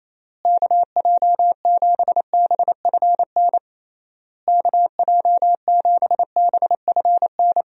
Morse code, heard as a keyed tone, about 27 wpm, 700 Hz. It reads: KJ7BFN KJ7BFN
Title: USA Call Signs - 6 Characters - Repeated 3x - 28wpm